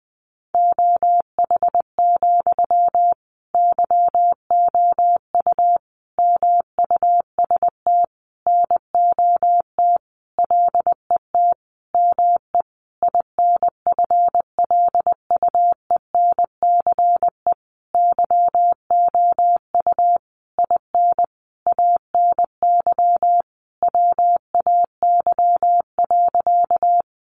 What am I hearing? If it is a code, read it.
Morse code, heard as a keyed tone, about 20 wpm, 700 Hz. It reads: OH, YOU MUST NOT LET ME INFLUENCE YOU IN ANY WAY.
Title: Sentences from Top 1100 Words 20wpm